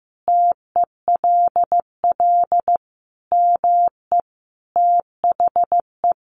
Morse code, heard as a keyed tone, about 15 wpm, 700 Hz. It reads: TELL ME THE
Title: Sentences from Top 1400 Words 15wpm